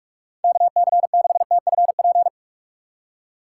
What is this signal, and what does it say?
Morse code, heard as a keyed tone, about 46 wpm, 700 Hz. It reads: KC6TFP
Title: USA Call Signs - 6 Characters - 2x Character Spacing - 45wpm